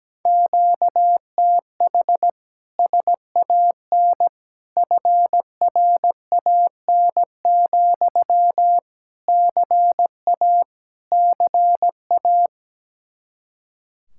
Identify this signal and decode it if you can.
Morse code, heard as a keyed tone, about 17 wpm, 700 Hz. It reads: QTH SAN FRAN, CA CA
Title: Speed-Racing Common QSO Phrases Encore - 17wpm